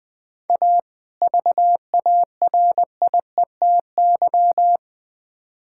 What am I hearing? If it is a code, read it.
Morse code, heard as a keyed tone, about 20 wpm, 700 Hz. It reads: A VARIETY
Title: Build A Phrase - 3-Word Phrases - 20wpm